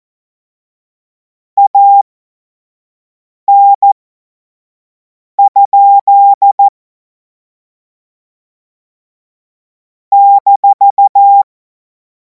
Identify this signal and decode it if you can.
Morse code, heard as a keyed tone, about 14 wpm, spaced as for 5 wpm, 800 Hz. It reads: AN? -
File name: Morsekurs / Witze / Witz 05. Mai 2026 25 F70